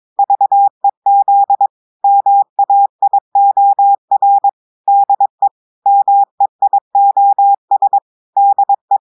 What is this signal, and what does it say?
Morse code, heard as a keyed tone, about 22 wpm, 800 Hz. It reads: VEZ MAIOR DE MEIOS DE
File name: audios / portugais / portugais texte en clair 2 vitesse 22 mots minutes